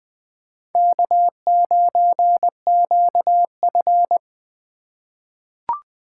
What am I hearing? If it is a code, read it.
Morse code, heard as a keyed tone, about 20 wpm, 700 Hz. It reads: K9QF
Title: USA Call Signs - 4 Characters - 2x Character Spacing - 20wpm